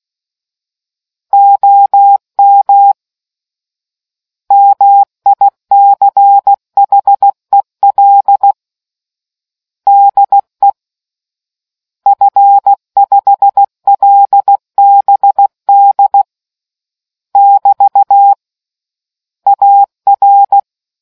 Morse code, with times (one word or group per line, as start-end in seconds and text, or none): 1.3-2.9 OM
4.5-8.5 MICHEL
9.9-10.7 DE
12.1-16.2 F5LBD
17.4-18.3 <BT>
19.5-20.6 AR